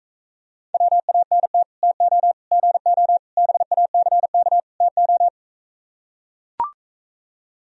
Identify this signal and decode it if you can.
Morse code, heard as a keyed tone, about 44 wpm, 700 Hz. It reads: WANT TO GO BACK TO